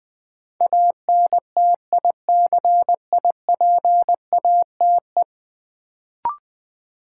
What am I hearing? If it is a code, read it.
Morse code, heard as a keyed tone, about 20 wpm, 700 Hz. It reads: ANTICIPATE
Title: Sentences from Top 1500 Words - New Word Drill - 20wpm